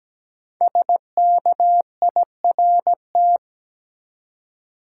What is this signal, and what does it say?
Morse code, heard as a keyed tone, about 17 wpm, 700 Hz. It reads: SKIRT
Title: Five Letter Words - 17wpm